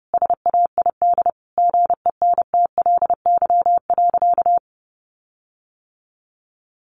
Morse code, heard as a keyed tone, about 30 wpm, 700 Hz. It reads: SAID GENTLY.
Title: Sentences from Top 1500 Words 30wpm